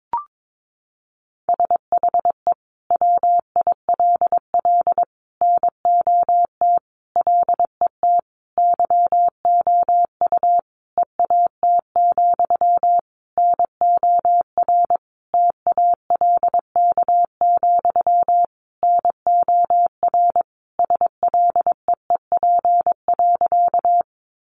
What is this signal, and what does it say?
Morse code, heard as a keyed tone, about 22 wpm, 700 Hz. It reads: SHE WILL NOT LET YOU EAT, NOR TALK, NOR SLEEP.
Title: Sentences from Top 1500 Words 22wpm